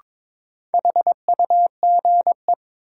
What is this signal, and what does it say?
Morse code, heard as a keyed tone, about 22 wpm, 700 Hz. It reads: HUGE
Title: Sentences from Top 1100 Words - New Word Drill Alt - 22wpm